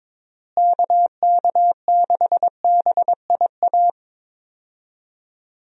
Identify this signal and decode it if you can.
Morse code, heard as a keyed tone, about 22 wpm, 700 Hz. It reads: KK6BIA
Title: USA Call Signs - 6 Characters - Repeated 3x - 22wpm